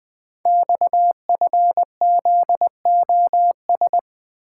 Morse code, heard as a keyed tone, about 20 wpm, 700 Hz. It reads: XFZOS
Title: Code Groups - 5 Letters - 20wpm